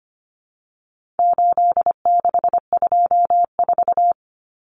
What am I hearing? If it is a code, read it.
Morse code, heard as a keyed tone, about 25 wpm, 700 Hz. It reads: 8624
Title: Common Cut Numbers - 4-Digit 25wpm